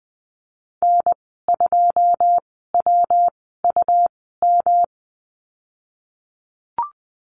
Morse code, heard as a keyed tone, about 20 wpm, spaced as for 15 wpm, 700 Hz. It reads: N2WUM
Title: 219 Lesson 35g - Number 0 - Call Signs 20wpm